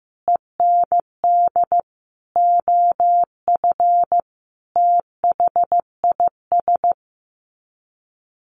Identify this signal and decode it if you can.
Morse code, heard as a keyed tone, about 15 wpm, 700 Hz. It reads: END OF THIS
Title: Sets of 5 Words Encore 15wpm